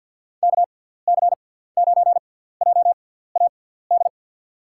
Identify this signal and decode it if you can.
Morse code, heard as a keyed tone, about 50 wpm, spaced as for 17 wpm, 700 Hz. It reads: KC9JAD
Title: USA Call Signs - 6 Characters - 6x Character Spacing - 50wpm